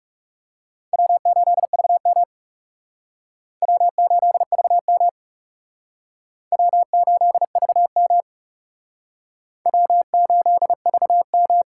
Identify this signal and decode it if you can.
Morse code, heard as a keyed tone, about 39 wpm, 700 Hz. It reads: W8VM W8VM W8VM W8VM